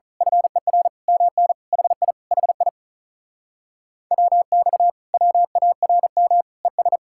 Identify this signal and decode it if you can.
Morse code, heard as a keyed tone, about 33 wpm, 700 Hz. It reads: FER MN HI HI WX WARM ES
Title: Speed-Racing Common QSO Phrases Encore - 35wpm